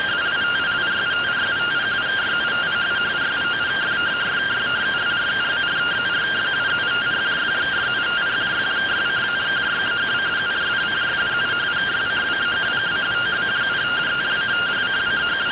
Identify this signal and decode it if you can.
Two-tone teleprinter signal, about 170 Hz shift.